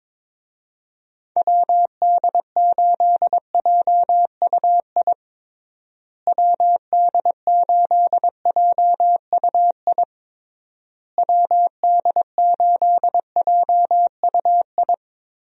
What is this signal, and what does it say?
Morse code, heard as a keyed tone, about 22 wpm, 700 Hz. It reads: WD8JUI WD8JUI WD8JUI
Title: USA Call Signs - 6 Characters - Repeated 3x - 22wpm